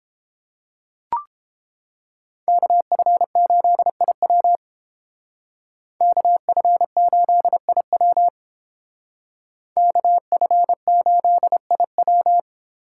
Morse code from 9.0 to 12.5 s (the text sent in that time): KF8IW